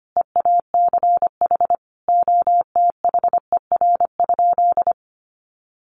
Morse code, heard as a keyed tone, about 25 wpm, 700 Hz. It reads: EACH OTHER?
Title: Sentences from Top 200 Words 25wpm